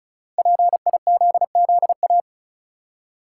Morse code, heard as a keyed tone, about 35 wpm, 700 Hz. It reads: PIZZA
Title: Five Letter Words - 35wpm